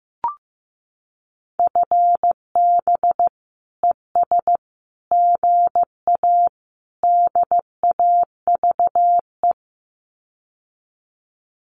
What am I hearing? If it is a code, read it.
Morse code, heard as a keyed tone, about 15 wpm, 700 Hz. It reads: FB ES GA DAVE
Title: Common QSO Phrases Encore 15wpm